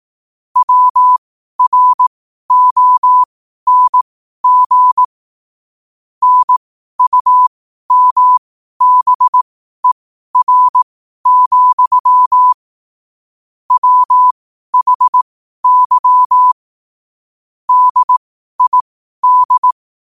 Morse code, heard as a keyed tone, about 18 wpm, spaced as for 12 wpm, 1000 Hz. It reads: WRONG NUMBER, WHY DID
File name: qotdfiles / 12wpm